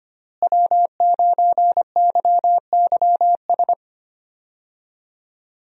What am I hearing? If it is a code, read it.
Morse code, heard as a keyed tone, about 25 wpm, 700 Hz. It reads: W9YYS
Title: USA Call Signs - 5 Characters - Repeated 2x - 25wpm